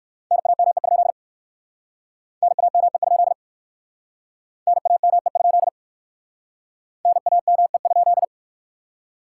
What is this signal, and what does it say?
Morse code, heard as a keyed tone, about 64 wpm, 700 Hz. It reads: NAME? NAME? NAME? NAME?